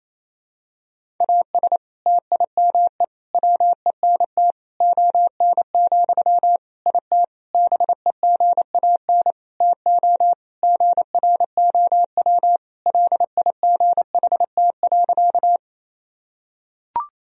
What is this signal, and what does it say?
Morse code, heard as a keyed tone, about 28 wpm, 700 Hz. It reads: AS TIME WENT ON, IT BEGAN TO GROW LIGHT.
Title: Sentences from Top 1000 Words - Review Part 1 - 28wpm